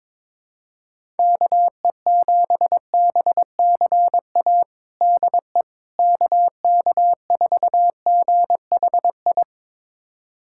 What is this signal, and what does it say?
Morse code, heard as a keyed tone, about 22 wpm, 700 Hz. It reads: KE7BCA DE KK4GHI
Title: USA Callsign DE Callsign 22wpm